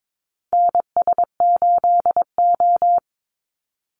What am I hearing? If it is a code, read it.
Morse code, heard as a keyed tone, about 22 wpm, 700 Hz. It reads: NS8O
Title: Call Signs - 22wpm